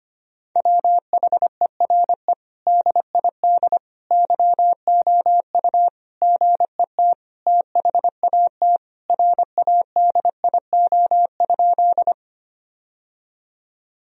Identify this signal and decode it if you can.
Morse code, heard as a keyed tone, about 25 wpm, 700 Hz. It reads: WHERE DID YOU GET THAT RADIO?
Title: Sentences from Top 1000 Words - Review Part 3 - 25wpm